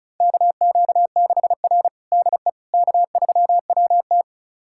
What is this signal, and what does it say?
Morse code, heard as a keyed tone, about 36 wpm, 700 Hz. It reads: KQ6R DE K3WT
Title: USA Callsign DE Callsign - 4 Characters - 35wpm